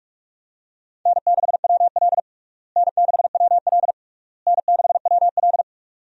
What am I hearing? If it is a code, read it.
Morse code, heard as a keyed tone, about 47 wpm, 700 Hz. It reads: N6WL N6WL N6WL
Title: USA Call Signs - 4 Characters - Repeated 3x - 45wpm